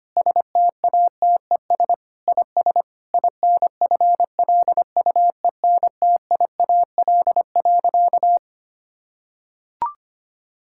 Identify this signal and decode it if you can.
Morse code, heard as a keyed tone, about 25 wpm, 700 Hz. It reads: STATES IS INFLUENTIAL.
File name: practice / 25wpm / Sentences with 4-Word Phrases - 25wpm